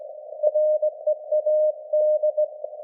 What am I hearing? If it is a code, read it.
Morse code, heard as a keyed tone, about 19 wpm, 610 Hz. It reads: READ
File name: laengstwellenempfang / SDRSharp 17kHz AF